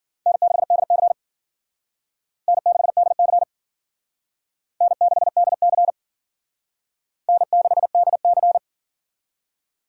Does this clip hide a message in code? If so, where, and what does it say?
Yes: Morse code from 0.3 to 8.6 s, heard as a keyed tone, about 51 wpm, 700 Hz. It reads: N6DC N6DC N6DC N6DC